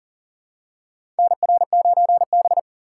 Morse code, heard as a keyed tone, about 40 wpm, 700 Hz. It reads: NR9B
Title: USA Call Signs - 4 Characters - 6x Character Spacing - 40wpm